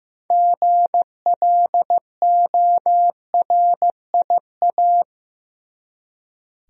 Morse code, heard as a keyed tone, about 15 wpm, 700 Hz. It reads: GLORIA
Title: USA Common Names - 15wpm